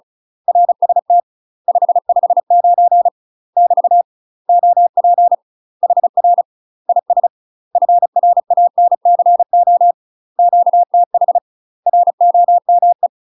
Morse code, heard as a keyed tone, about 36 wpm, 700 Hz. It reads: RST 559 <BT> OP HR IS FRANCO QTH ROME
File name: UK MASTER QSOs TRAINING / 35WPM / UK MASTER QSOs 0023